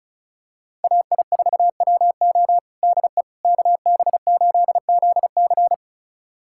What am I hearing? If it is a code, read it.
Morse code, heard as a keyed tone, about 34 wpm, 700 Hz. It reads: AI4WO DE KB8ZC
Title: USA Callsign DE Callsign - 5 Characters - 35wpm